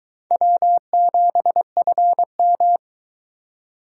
Morse code, heard as a keyed tone, about 22 wpm, 700 Hz. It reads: W7FM